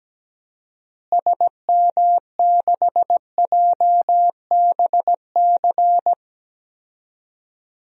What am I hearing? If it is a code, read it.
Morse code, heard as a keyed tone, about 17 wpm, 700 Hz. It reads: SM6JBC
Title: DXCC-100 Call Signs - 6 Characters - Repeated 2x - 17wpm